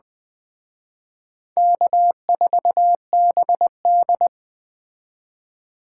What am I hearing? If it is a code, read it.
Morse code, heard as a keyed tone, about 20 wpm, 700 Hz. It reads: K4BD